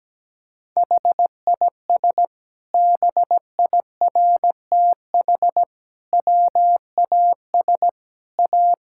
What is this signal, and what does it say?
Morse code, heard as a keyed tone, about 17 wpm, 700 Hz. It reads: HIS BIRTH WAS A
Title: Sentences from Top 1000 Words - Review Part 5 - 17wpm